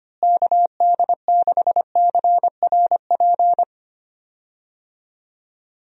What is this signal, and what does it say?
Morse code, heard as a keyed tone, about 25 wpm, 700 Hz. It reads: KD6CRP
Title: USA Call Signs - 6 Characters - Repeated 3x - 25wpm